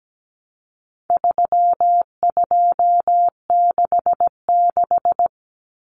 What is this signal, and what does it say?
Morse code, heard as a keyed tone, about 17 wpm, 700 Hz. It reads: 3266
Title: Numbers - 4-Digit 17wpm